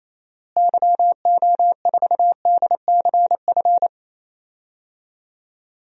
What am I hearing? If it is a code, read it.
Morse code, heard as a keyed tone, about 27 wpm, 700 Hz. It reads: YO4DCF
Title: DXCC-100 Call Signs - 6 Characters - Repeated 2x - 28wpm